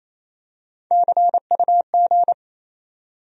Morse code, heard as a keyed tone, about 28 wpm, 700 Hz. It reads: CUG
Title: USA Call Sign Postfixes - 3 Letters 28wpm